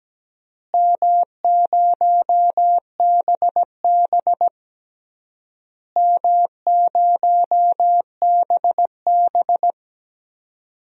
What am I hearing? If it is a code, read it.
Morse code, heard as a keyed tone, about 17 wpm, 700 Hz. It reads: M0BB M0BB
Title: DXCC-100 Call Signs - 4 Characters - Repeated 3x - 17wpm